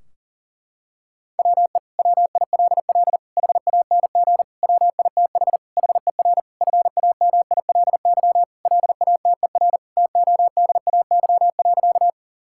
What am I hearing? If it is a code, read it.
Morse code, heard as a keyed tone, about 37 wpm, 700 Hz. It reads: WE WILL HANG WITH HER FAMILY LATER TODAY.
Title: Sentences from Top 1600 Words 40wpm